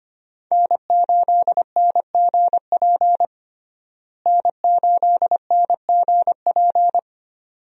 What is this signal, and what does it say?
Morse code, heard as a keyed tone, about 25 wpm, 700 Hz. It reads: N8NGP N8NGP
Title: USA Call Signs - 5 Characters - Repeated 2x - 25wpm